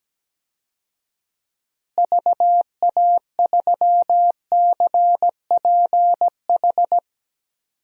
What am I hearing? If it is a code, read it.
Morse code, heard as a keyed tone, about 17 wpm, 700 Hz. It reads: VA3CPH
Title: DXCC-100 Call Signs - 6 Characters - Repeated 3x - 17wpm